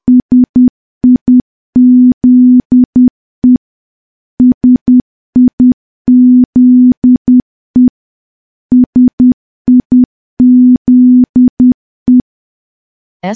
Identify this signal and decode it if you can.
Morse code, heard as a keyed tone, about 10 wpm, 250 Hz. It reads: SIZE SIZE SIZE
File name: morse / morse-training-10WPM